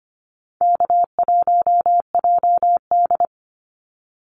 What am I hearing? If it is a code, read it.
Morse code, heard as a keyed tone, about 25 wpm, 700 Hz. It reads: K1JD
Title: NA SOTA Activator Callsigns 25wpm